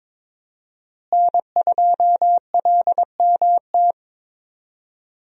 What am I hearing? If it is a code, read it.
Morse code, heard as a keyed tone, about 22 wpm, 700 Hz. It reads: N2LMT
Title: USA Call Signs - 5 Characters - Repeated 3x - 22wpm